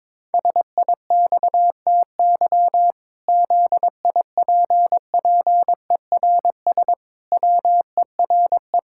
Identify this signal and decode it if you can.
Morse code, heard as a keyed tone, about 22 wpm, 700 Hz. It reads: SIXTY ZIPPERS WERE
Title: Pangram Challenge v2 - 22wpm